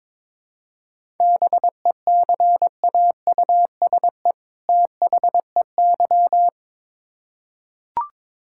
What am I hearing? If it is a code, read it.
Morse code, heard as a keyed tone, about 22 wpm, 700 Hz. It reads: BECAUSE THEY
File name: Build A Phrase - 2-Word Phrases - 22wpm